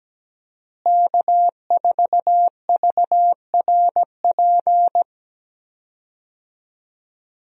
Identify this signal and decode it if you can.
Morse code, heard as a keyed tone, about 17 wpm, 700 Hz. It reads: K4VRP
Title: USA Call Signs - 5 Characters - Repeated 2x - 17wpm